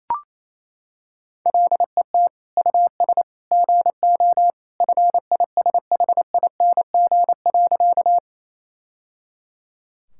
Morse code, heard as a keyed tone, about 28 wpm, 700 Hz. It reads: LET US GO FISHING.
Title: Sentences from Top 1000 Words - Review Part 5 - 28wpm